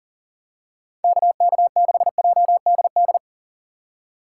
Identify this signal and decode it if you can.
Morse code, heard as a keyed tone, about 39 wpm, 700 Hz. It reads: KK6JDD
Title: USA Call Signs - 6 Characters - 2x Character Spacing - 40wpm